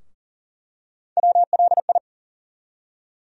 Morse code, heard as a keyed tone, about 40 wpm, 700 Hz. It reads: WLI
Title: USA Call Sign Postfixes - 3 Letters 40wpm